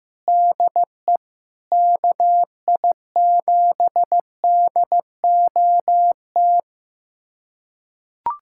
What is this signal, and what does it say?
Morse code, heard as a keyed tone, about 16 wpm, 700 Hz. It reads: DE KI7DOT
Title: USA Callsign DE Callsign - 6 Characters - 15wpm